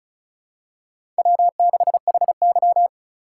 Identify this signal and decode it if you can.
Morse code, heard as a keyed tone, about 34 wpm, 700 Hz. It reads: W6HY
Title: USA Call Signs - 4 Characters - 3x Character Spacing - 35wpm